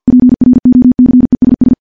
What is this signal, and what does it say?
Morse code, heard as a keyed tone, about 50 wpm, 250 Hz. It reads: PROCESS
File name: morse / morse-training-50WPM